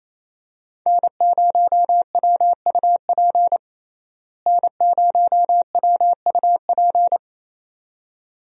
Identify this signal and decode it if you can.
Morse code, heard as a keyed tone, about 28 wpm, 700 Hz. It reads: N0WUP N0WUP
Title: USA Call Signs - 5 Characters - Repeated 2x - 28wpm